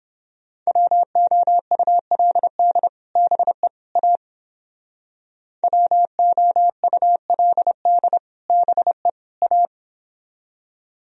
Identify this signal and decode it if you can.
Morse code, heard as a keyed tone, about 29 wpm, 700 Hz. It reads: WOULD BE A WOULD BE A